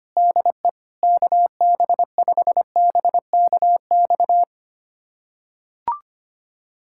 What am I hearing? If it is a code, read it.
Morse code, heard as a keyed tone, about 25 wpm, 700 Hz. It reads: DE KB5BKX
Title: USA Callsign DE Callsign 25wpm